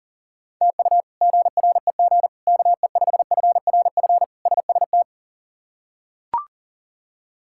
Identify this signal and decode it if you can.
Morse code, heard as a keyed tone, about 37 wpm, 700 Hz. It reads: TU GREG KE5FRF SST
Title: K1USN SST QSO Practice - 40wpm